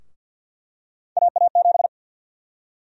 Morse code, heard as a keyed tone, about 52 wpm, 700 Hz. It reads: AA7
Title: USA Call Sign Prefixes - 50wpm